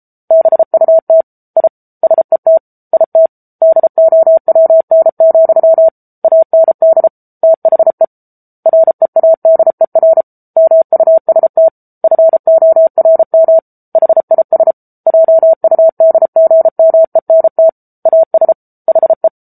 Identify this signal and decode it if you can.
Morse code, heard as a keyed tone, about 34 wpm, 650 Hz. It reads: BUT I SET IT DOWN, AND THE READER MUST FORM HIS JUDGMENT AS HE
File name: Book Two / 07-Chapter 7 - 33 WPM